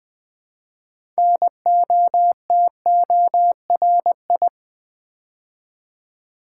Morse code, heard as a keyed tone, about 20 wpm, 700 Hz. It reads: NOTORI
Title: Build A Word - 3 Letters At A Time - 9-Letter Words - 20wpm